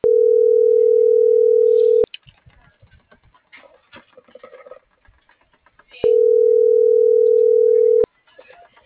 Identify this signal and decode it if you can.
Ringback tone (North America), 440 + 480 Hz.